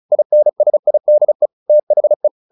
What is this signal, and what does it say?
Morse code, heard as a keyed tone, about 34 wpm, 600 Hz. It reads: INSIDE THE
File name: the old man and the sea 2 35wpm